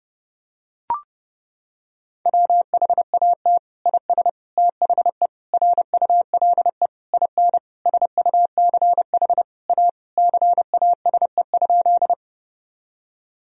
Morse code, heard as a keyed tone, about 30 wpm, 700 Hz. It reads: WHAT IS THE RULE IN SUCH A CASE?
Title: Sentences from Top 1000 Words - Review Part 2 - 30wpm